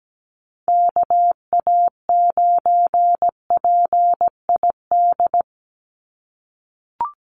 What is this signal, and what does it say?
Morse code, heard as a keyed tone, about 17 wpm, 700 Hz. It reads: KA9PID
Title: USA Call Signs 17wpm